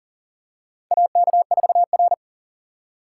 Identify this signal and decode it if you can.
Morse code, heard as a keyed tone, about 40 wpm, 700 Hz. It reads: AK4R
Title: USA Call Signs - 4 Characters - 6x Character Spacing - 40wpm